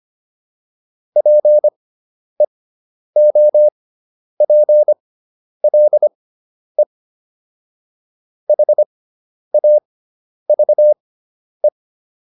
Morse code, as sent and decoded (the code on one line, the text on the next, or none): .--. --- .--. .-.. / .... .- ...- .
PEOPLE HAVE